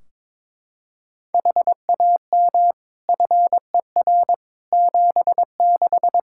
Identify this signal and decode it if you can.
Morse code, heard as a keyed tone, about 21 wpm, 700 Hz. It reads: HAM FER 76
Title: Practice QSO - Exchange 3 - More Info - 22wpm